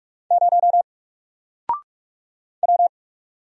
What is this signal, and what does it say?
Morse code, heard as a keyed tone, about 44 wpm, 700 Hz. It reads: 0 W
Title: Single Letter-Number Alt 45wpm